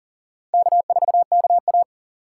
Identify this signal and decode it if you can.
Morse code, heard as a keyed tone, about 37 wpm, 700 Hz. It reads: K4KA
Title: USA Call Signs - 6x Character Spacing - 40wpm